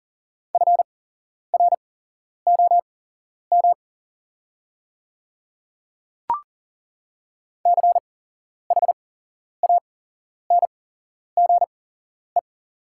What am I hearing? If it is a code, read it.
Morse code, heard as a keyed tone, about 40 wpm, spaced as for 11 wpm, 700 Hz. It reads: FROM CHANGE
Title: 8x Character Spacing - Top 200 Words - 40wpm